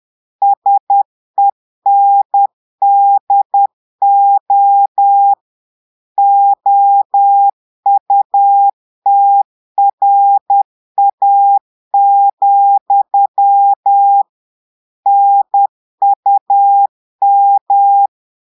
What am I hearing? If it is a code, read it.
Morse code, heard as a keyed tone, about 10 wpm, 800 Hz. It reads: SENDO OUTRA, NUM
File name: Portugais - texte en clair 1 - vitesse - 10 mots minute